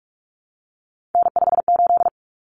Morse code, heard as a keyed tone, about 44 wpm, 700 Hz. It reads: N58
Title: Common Cut Numbers - 3-Digit 45wpm